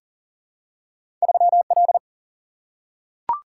Morse code, heard as a keyed tone, about 39 wpm, 700 Hz. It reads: 3L
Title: Single-Letter Single-Number 40wpm